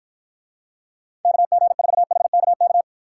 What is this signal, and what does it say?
Morse code, heard as a keyed tone, about 49 wpm, 700 Hz. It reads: KG4HKK